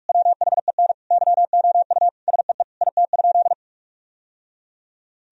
Morse code, heard as a keyed tone, about 41 wpm, 700 Hz. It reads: WHEN YOU SEE IT?
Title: Sentences from Top 1000 Words - Review Part 4 - 45wpm